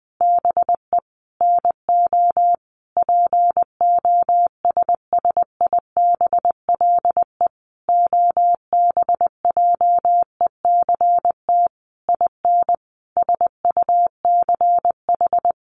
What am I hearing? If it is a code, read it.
Morse code, heard as a keyed tone, about 20 wpm, 700 Hz. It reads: BE NO POSSIBLE OBJECT IN SUCH
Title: Sentences from Top 1500 Words 20wpm